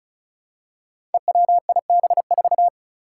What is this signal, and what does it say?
Morse code, heard as a keyed tone, about 36 wpm, 700 Hz. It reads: EWIB4
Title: Five Challenging Letter-Number Combinations 35wpm